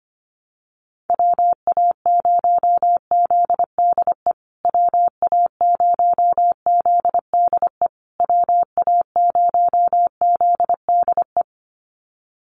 Morse code, heard as a keyed tone, about 25 wpm, 700 Hz. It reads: WA0ZDE WA0ZDE WA0ZDE
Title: USA Call Signs Repeated 3x 25wpm